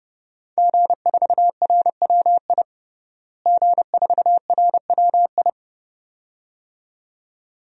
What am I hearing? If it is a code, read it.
Morse code, heard as a keyed tone, about 32 wpm, 700 Hz. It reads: G4RWI G4RWI
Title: DXCC-100 Call Signs - 5 Characters - Repeated 2x - 30wpm